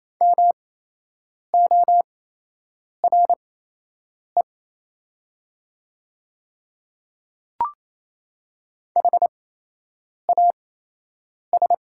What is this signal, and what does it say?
Morse code, heard as a keyed tone, about 28 wpm, spaced as for 8 wpm, 700 Hz. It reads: MORE HAS
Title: 8x Character Spacing - Top 100 Words - 28wpm